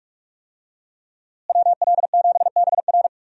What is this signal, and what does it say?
Morse code, heard as a keyed tone, about 47 wpm, 700 Hz. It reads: WL7BR
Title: USA Call Signs - 5 Characters - Repeated 3x - 45wpm